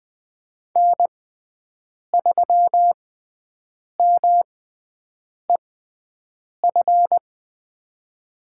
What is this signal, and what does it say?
Morse code, heard as a keyed tone, about 21 wpm, spaced as for 7 wpm, 700 Hz. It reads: N3MEF
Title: USA Call Signs - 5 Characters - 6x Character Spacing - 20wpm